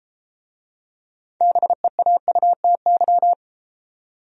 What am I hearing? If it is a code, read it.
Morse code, heard as a keyed tone, about 33 wpm, 700 Hz. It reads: BEAUTY